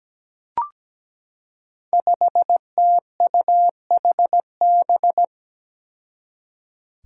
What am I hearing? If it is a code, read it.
Morse code, heard as a keyed tone, about 17 wpm, 700 Hz. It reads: 5TUHB
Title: Five Challenging Letter-Number Combinations 17wpm